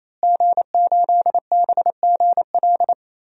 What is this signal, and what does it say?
Morse code, heard as a keyed tone, about 27 wpm, 700 Hz. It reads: G8BGL
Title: DXCC-100 Call Signs - 5 Characters - Repeated 3x - 28wpm